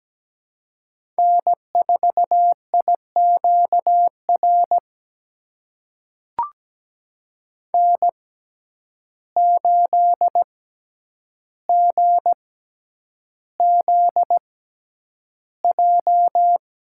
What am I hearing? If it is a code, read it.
Morse code, heard as a keyed tone, about 17 wpm, spaced as for 9 wpm, 700 Hz. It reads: N4IQR N8GZJ
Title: USA Call Signs - 5 Characters - 6x Character Spacing - 17wpm